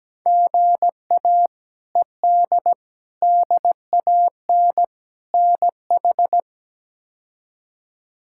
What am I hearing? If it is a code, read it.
Morse code, heard as a keyed tone, about 17 wpm, 700 Hz. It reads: GA ED DAN NH
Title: K1USN SST QSO Practice - 17wpm